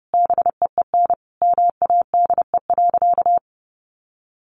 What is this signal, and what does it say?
Morse code, heard as a keyed tone, about 30 wpm, 700 Hz. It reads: BEEN MADE.
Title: Sentences from Top 1200 Words 30wpm